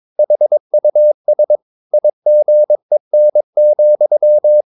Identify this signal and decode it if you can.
Morse code, heard as a keyed tone, about 22 wpm, 600 Hz. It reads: HUS IGEN,
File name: Ferie QRS 22wpm